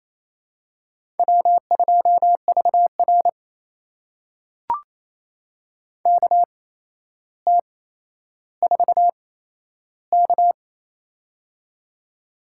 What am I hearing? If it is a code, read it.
Morse code, heard as a keyed tone, about 28 wpm, spaced as for 12 wpm, 700 Hz. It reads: W2VR KT4K
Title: USA Call Signs - 4 Characters - 8x Character Spacing - 28wpm